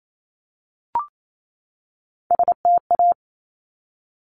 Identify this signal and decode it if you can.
Morse code, heard as a keyed tone, about 28 wpm, 700 Hz. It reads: STA
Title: Top Three Letter Combinations in Words 28wpm